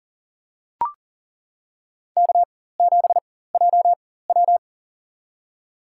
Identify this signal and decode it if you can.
Morse code, heard as a keyed tone, about 40 wpm, spaced as for 19 wpm, 700 Hz. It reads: K7JW
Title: USA Call Signs - 4 Characters - 4x Character Spacing - 40wpm